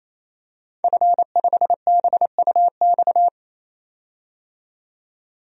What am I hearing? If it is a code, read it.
Morse code, heard as a keyed tone, about 28 wpm, 700 Hz. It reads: F5BUX
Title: DXCC-100 Call Signs - 5 Characters - 28wpm